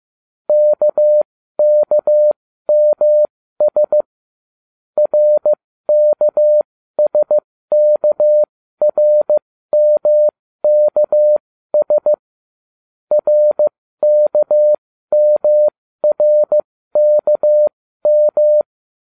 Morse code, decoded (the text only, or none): KKMS RKSKRMKS RKMRKM